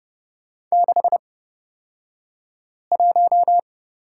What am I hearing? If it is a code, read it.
Morse code, heard as a keyed tone, about 31 wpm, spaced as for 5 wpm, 700 Hz. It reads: 61
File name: Lesson 38d - Letter q - Two Characters 30wpm